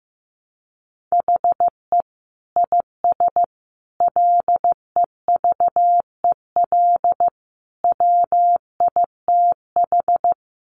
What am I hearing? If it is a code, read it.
Morse code, heard as a keyed tone, about 15 wpm, 700 Hz. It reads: HE IS LEVEL WITH
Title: Sentences from Top 700 Words 15wpm